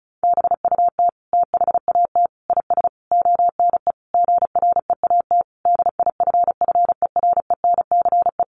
Morse code, heard as a keyed tone, about 34 wpm, 700 Hz. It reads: BUT THAT IS ONE GREAT DIFFERENCE
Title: Sentences from Top 800 Words 35wpm